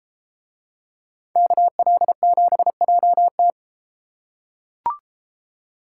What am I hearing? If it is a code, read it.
Morse code, heard as a keyed tone, about 32 wpm, 700 Hz. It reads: KL7JT